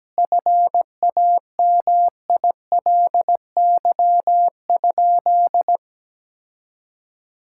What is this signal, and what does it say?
Morse code, heard as a keyed tone, about 17 wpm, 700 Hz. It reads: FAMILY?
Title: Sentences from Top 1000 Words - Review Part 6 - 17wpm